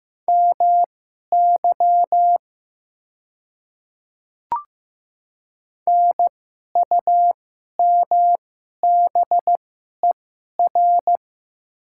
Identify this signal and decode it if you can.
Morse code, heard as a keyed tone, about 15 wpm, spaced as for 11 wpm, 700 Hz. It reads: MY NUMBER
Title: 2x Character Spacing - Top 100 Words - 15wpm